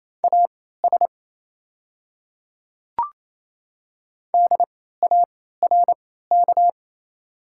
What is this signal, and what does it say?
Morse code, heard as a keyed tone, about 28 wpm, spaced as for 16 wpm, 690 Hz. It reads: AS DARK
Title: 3x Character Spacing - Top 500 Words - 28wpm